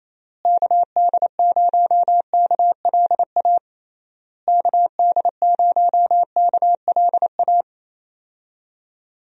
Morse code, heard as a keyed tone, about 27 wpm, 700 Hz. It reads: KD0KLA KD0KLA
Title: USA Call Signs - 6 Characters - Repeated 3x - 28wpm